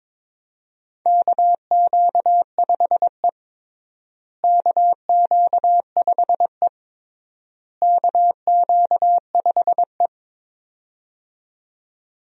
Morse code, heard as a keyed tone, about 22 wpm, 700 Hz. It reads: KQ5E KQ5E KQ5E
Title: USA Call Signs - 4 Characters - Repeated 3x - 22wpm